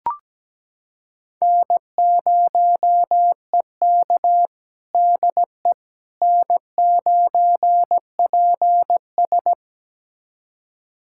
Morse code, heard as a keyed tone, about 17 wpm, 700 Hz. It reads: N0EK DE N9PS
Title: USA Callsign DE Callsign - 4 Characters - 17wpm